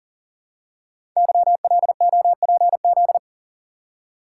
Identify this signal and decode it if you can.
Morse code, heard as a keyed tone, about 38 wpm, 690 Hz. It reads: YLOPZ
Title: Code Groups - 5 Letters - 40wpm